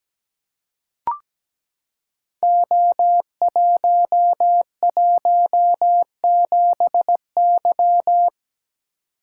Morse code, heard as a keyed tone, about 17 wpm, 700 Hz. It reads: O117Y
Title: Code Groups - 5 Letter:Numbers - 17wpm